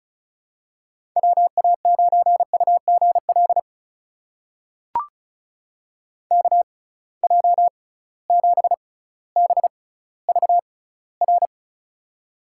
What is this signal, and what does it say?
Morse code, heard as a keyed tone, about 36 wpm, spaced as for 18 wpm, 700 Hz. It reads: WA9UGL KJ7BVR
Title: USA Call Signs - 6 Characters - 6x Character Spacing - 35wpm